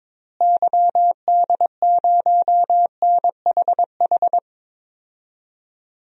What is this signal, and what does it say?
Morse code, heard as a keyed tone, about 22 wpm, 700 Hz. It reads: YD0NHH
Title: DXCC-100 Call Signs - 6 Characters - Repeated 2x - 22wpm